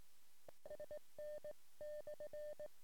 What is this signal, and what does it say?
Morse code, heard as a keyed tone, about 18 wpm, 600 Hz. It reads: SN/
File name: sm7lcb / 08 sm5le-p jo98ax 23cm